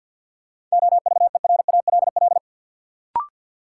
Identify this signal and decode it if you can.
Morse code, heard as a keyed tone, about 53 wpm, 700 Hz. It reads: OVERALL
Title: Sentences from Top 1000 Words - Review Part 7 - New Word Drill - 50wpm